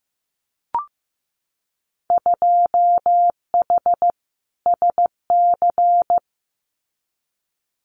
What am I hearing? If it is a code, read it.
Morse code, heard as a keyed tone, about 15 wpm, 700 Hz. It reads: 2H SC
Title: Winter Field Day Exchanges 15wpm